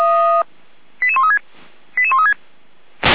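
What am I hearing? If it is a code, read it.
Five-tone selective call (ZVEI), heard as quick runs of five tones, tones of about 70 ms. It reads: 88136 88136
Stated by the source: Received by SDR, ZVEI five-tone call from the feuerwehr-orlamuende.org (Orlamünde fire brigade) audio server.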